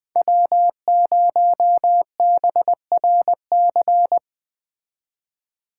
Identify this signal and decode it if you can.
Morse code, heard as a keyed tone, about 20 wpm, 700 Hz. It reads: W0BRC
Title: Speed Racing - USA 1x3 Call Signs 17wpm